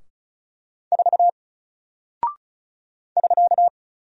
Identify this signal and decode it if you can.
Morse code, heard as a keyed tone, about 36 wpm, 700 Hz. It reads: 4 <SK>
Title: Single Character (Letter:Number:Punctuation:Prosign) 35wpm